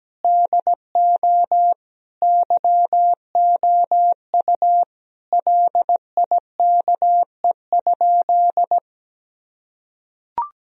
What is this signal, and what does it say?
Morse code, heard as a keyed tone, about 17 wpm, 700 Hz. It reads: DO YOU LIKE?
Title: Sentences from Top 1000 Words - Review Part 3 - 17wpm